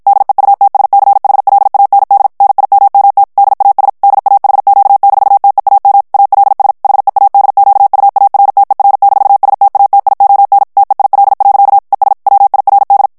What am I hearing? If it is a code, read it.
Morse code, heard as a keyed tone, about 55 wpm, 800 Hz. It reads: DEUTSCHLAND NIMMT DAS DAHQ-TEAM ALS HEADQUARTER-STATION TEIL. ES WIRD
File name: morse mp3 / ALLEMAND texte clair 2 vitesse 50 mots minute